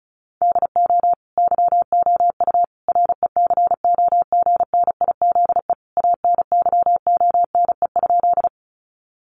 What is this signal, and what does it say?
Morse code, heard as a keyed tone, about 34 wpm, 700 Hz. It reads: DO YOU RECOGNIZE ANYONE?
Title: Sentences from Top 1400 Words 35wpm